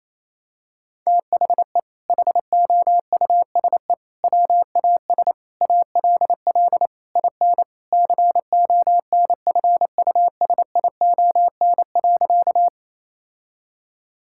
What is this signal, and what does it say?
Morse code, heard as a keyed tone, about 28 wpm, 700 Hz. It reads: THE HOUSE WAS ALL IN CONFUSION.
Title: Sentences from Top 1600 Words 28wpm